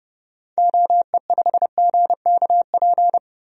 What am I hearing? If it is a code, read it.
Morse code, heard as a keyed tone, about 32 wpm, 700 Hz. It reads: OE5GKP
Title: DXCC-100 Call Signs - 6 Characters - 30wpm